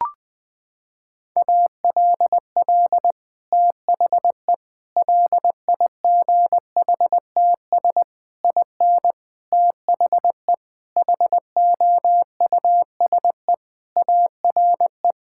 Morse code, heard as a keyed tone, about 20 wpm, 700 Hz. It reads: ALL THE LIGHTS IN THE HOUSE ARE
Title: Sentences with 2-Word Phrases - 20wpm